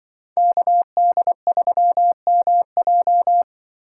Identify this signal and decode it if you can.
Morse code, heard as a keyed tone, about 24 wpm, 700 Hz. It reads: KD3MJ
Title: Speed Racing - USA 2x2 Call Signs 20wpm